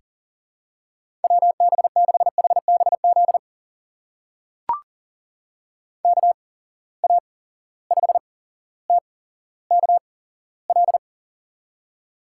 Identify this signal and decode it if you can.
Morse code, heard as a keyed tone, about 43 wpm, spaced as for 17 wpm, 700 Hz. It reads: WB6HBZ KA5TKL
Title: USA Call Signs - 6 Characters - 8x Character Spacing - 40wpm